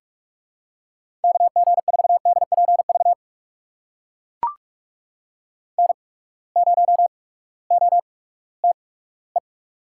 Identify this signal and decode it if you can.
Morse code, heard as a keyed tone, about 44 wpm, spaced as for 20 wpm, 700 Hz. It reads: KG4DPV N0OTE
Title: USA Call Signs - 8x Character Spacing - 45wpm